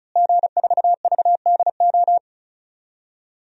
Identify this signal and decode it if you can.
Morse code, heard as a keyed tone, about 34 wpm, 700 Hz. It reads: G4VDO
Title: DXCC-100 Call Signs - 5 Characters - Repeated 3x - 35wpm